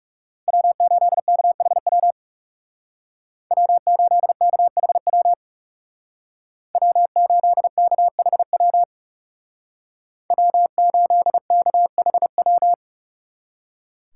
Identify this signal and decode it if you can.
Morse code, heard as a keyed tone, about 36 wpm, 700 Hz. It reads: W8KHW W8KHW W8KHW W8KHW
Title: Speed Racing - USA Call Signs 30wpm